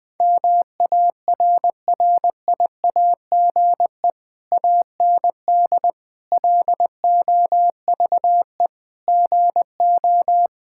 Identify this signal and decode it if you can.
Morse code, heard as a keyed tone, about 21 wpm, 700 Hz. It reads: MARRIAGE AND LOVE GO
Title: Sentences from Top 1000 Words - Review Part 5 - 20wpm